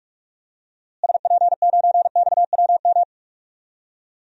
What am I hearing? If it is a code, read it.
Morse code, heard as a keyed tone, about 41 wpm, 700 Hz. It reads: SP9XWM
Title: DXCC-100 Call Signs - 6 Characters - Repeated 3x - 45wpm